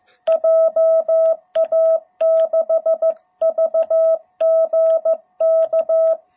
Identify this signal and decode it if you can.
Morse code, heard as a keyed tone, about 15 wpm, 660 Hz. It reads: JA6VGK